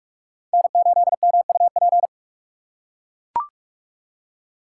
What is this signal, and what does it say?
Morse code, heard as a keyed tone, about 44 wpm, 690 Hz. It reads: N8MUP
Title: USA Call Signs - 5 Characters - 6x Character Spacing - 45wpm